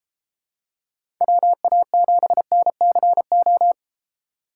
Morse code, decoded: WA7NCO

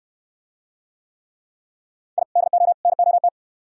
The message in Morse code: I DO NOT

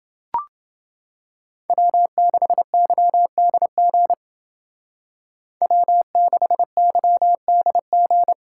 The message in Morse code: W6YDG W6YDG

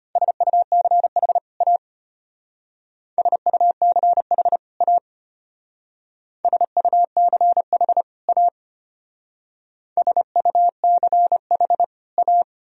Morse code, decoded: SUCH A SUCH A SUCH A SUCH A

